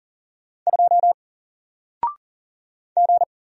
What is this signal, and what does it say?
Morse code, heard as a keyed tone, about 39 wpm, 700 Hz. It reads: 2 G
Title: Single Character (Letter:Number:Punctuation:Prosign) 40wpm